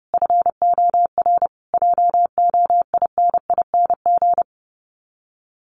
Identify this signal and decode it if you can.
Morse code, heard as a keyed tone, about 30 wpm, 700 Hz. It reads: FOR JOINING
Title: Sets of 5 Words Encore2 30wpm